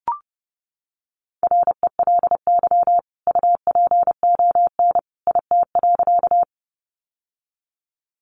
Morse code, heard as a keyed tone, about 30 wpm, 700 Hz. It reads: RELY UPON IT.
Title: Sentences from Top 1200 Words 30wpm